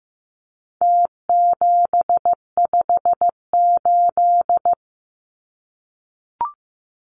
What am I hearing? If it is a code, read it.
Morse code, heard as a keyed tone, about 14 wpm, 700 Hz. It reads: T758
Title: Numbers - Contest Style - 4-Digit 15wpm